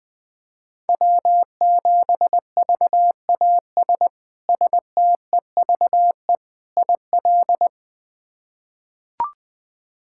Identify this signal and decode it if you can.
Morse code, heard as a keyed tone, about 20 wpm, 700 Hz. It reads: W7VAS STEVE IL
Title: K1USN SST QSO Practice - 20wpm